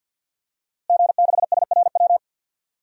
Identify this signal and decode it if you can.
Morse code, heard as a keyed tone, about 52 wpm, 700 Hz. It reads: G6SRW
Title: DXCC-100 Call Signs - 5 Characters - Repeated 2x - 50wpm